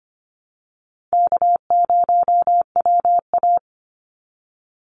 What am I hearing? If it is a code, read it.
Morse code, heard as a keyed tone, about 25 wpm, 700 Hz. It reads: K0WA
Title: Easy Call Signs - 25wpm